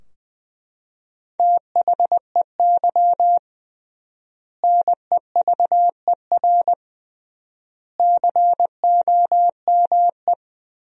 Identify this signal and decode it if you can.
Morse code, heard as a keyed tone, about 20 wpm, 700 Hz. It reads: THEY NEVER COME
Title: 3x Word-Spacing - Sentences from Top 300 Words - 20wpm